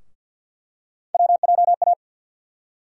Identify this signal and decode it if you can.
Morse code, heard as a keyed tone, about 52 wpm, 700 Hz. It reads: WJA
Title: USA Call Sign Postfixes - 3 Letters 50wpm